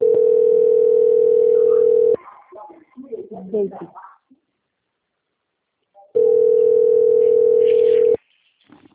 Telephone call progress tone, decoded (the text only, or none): ringback tone